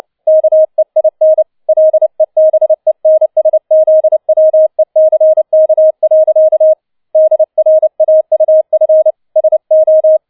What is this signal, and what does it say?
Morse code, heard as a keyed tone, about 29 wpm, 630 Hz. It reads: KEIN LEBENSZWECK. DRAUF SO
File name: 007 Letzter Streich 30wpm